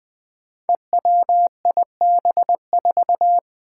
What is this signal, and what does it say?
Morse code, heard as a keyed tone, about 20 wpm, 700 Hz. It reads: EWIB4
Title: Five Challenging Letter-Number Combinations 20wpm